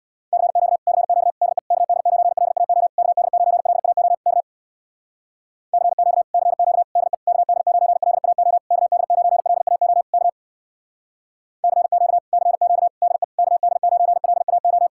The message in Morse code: CQ CQ DE KD9LAO KD9LAO K CQ CQ DE KD9LAO KD9LAO K CQ CQ DE KD9LAO

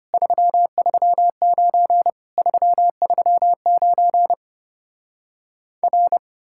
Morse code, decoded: 339 339 R

